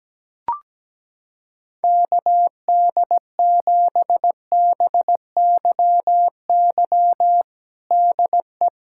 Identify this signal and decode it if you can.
Morse code, heard as a keyed tone, about 17 wpm, 700 Hz. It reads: KD7BYY DE
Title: USA Callsign DE Callsign - 6 Characters - 17wpm